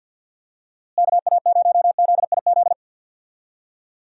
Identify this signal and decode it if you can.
Morse code, heard as a keyed tone, about 50 wpm, 700 Hz. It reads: KA0ZIZ
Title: USA Call Signs - 6 Characters - Repeated 3x - 50wpm